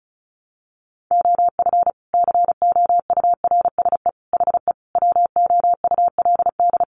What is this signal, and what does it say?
Morse code, heard as a keyed tone, about 34 wpm, 700 Hz. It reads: OF COURSE HE WOULD
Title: Sentences from Top 1200 Words 35wpm